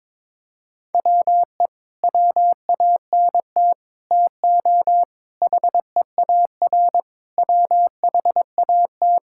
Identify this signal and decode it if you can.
Morse code, heard as a keyed tone, about 22 wpm, 700 Hz. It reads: WE WANT TO HEAR WHAT
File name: practice / 22wpm / Sets of 8 Words - 22wpm